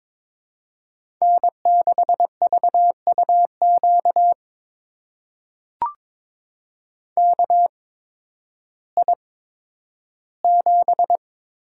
Morse code, heard as a keyed tone, about 22 wpm, spaced as for 12 wpm, 700 Hz. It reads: N6VUQ KI7
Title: USA Call Signs - 8x Character Spacing - 22wpm